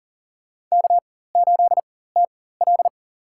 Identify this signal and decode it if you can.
Morse code, heard as a keyed tone, about 37 wpm, spaced as for 18 wpm, 700 Hz. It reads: K8TL
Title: USA Call Signs - 4 Characters - 4x Character Spacing - 40wpm